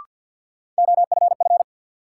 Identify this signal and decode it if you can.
Morse code, heard as a keyed tone, about 49 wpm, 700 Hz. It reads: OFF
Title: Sentences from Top 200 Words - New Word Drill - 50wpm